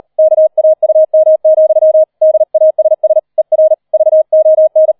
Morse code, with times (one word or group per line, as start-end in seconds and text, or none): 0.2-2.0 KAUM,
2.2-3.2 DASS
3.4-3.7 ER
3.9-4.9 VON